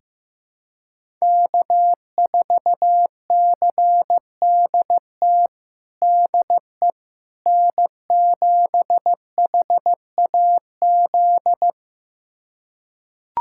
Morse code, heard as a keyed tone, about 16 wpm, 700 Hz. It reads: K4CDT DE N7HAZ
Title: USA Callsign DE Callsign - 5 Characters - 15wpm